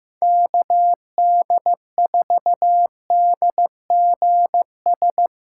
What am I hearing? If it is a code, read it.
Morse code, heard as a keyed tone, about 15 wpm, 700 Hz. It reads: KD4DGS
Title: USA Call Signs - 8x Character Spacing - 15wpm